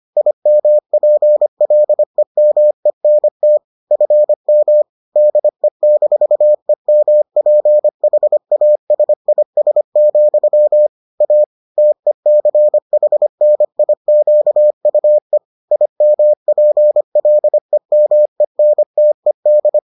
Morse code, heard as a keyed tone, about 25 wpm, 600 Hz. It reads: IMPLEMENT FM DE-EMPHASIS, A TECHNIQUE IMPLEMENTED